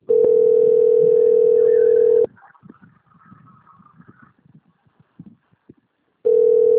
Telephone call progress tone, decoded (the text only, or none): ringback tone